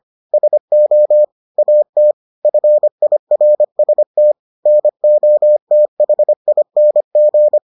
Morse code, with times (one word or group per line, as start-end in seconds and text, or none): 0.3-1.2 SO
1.6-2.1 AT
2.5-4.3 FIRST
4.7-7.6 NOTHING